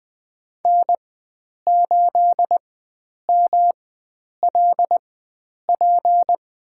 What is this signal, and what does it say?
Morse code, heard as a keyed tone, about 20 wpm, spaced as for 9 wpm, 700 Hz. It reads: N8MLP
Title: USA Call Signs - 5 Characters - 4x Character Spacing - 20wpm